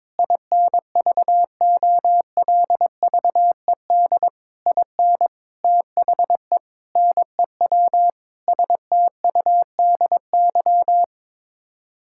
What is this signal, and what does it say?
Morse code, heard as a keyed tone, about 22 wpm, 700 Hz. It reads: INVOLVED IN THE NEW STUDY
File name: practice / 22wpm / Sets of 8 Words - 22wpm